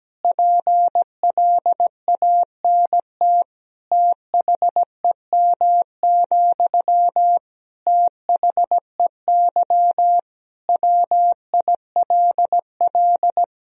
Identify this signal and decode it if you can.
Morse code, heard as a keyed tone, about 17 wpm, 700 Hz. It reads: PLANT THEM, THEY WILL
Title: Sentences from Top 1000 Words - Review Part 1 - 17wpm